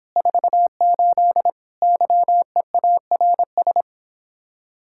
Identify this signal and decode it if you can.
Morse code, heard as a keyed tone, about 29 wpm, 700 Hz. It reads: 48 YEARS